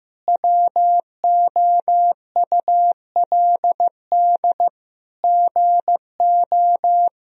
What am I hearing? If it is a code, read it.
Morse code, heard as a keyed tone, about 16 wpm, 700 Hz. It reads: WOULD GO
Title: Sentences from Top 1000 Words - Review Part 3 - 15wpm